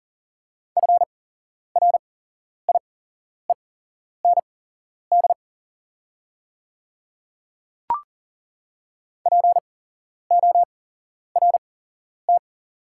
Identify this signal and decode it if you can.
Morse code, heard as a keyed tone, about 40 wpm, spaced as for 11 wpm, 690 Hz. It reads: FRIEND PORT
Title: 8x Character Spacing - Top 300 Words - 40wpm